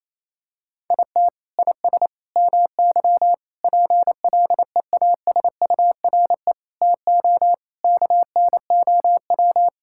Morse code, heard as a keyed tone, about 28 wpm, 700 Hz. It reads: IT IS MY PLEASURE TO KNOW
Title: Sentences from Top 1600 Words 28wpm